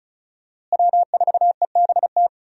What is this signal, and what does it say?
Morse code, heard as a keyed tone, about 36 wpm, 700 Hz. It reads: W4EBT
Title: USA Call Signs - 5 Characters - Repeated 3x - 35wpm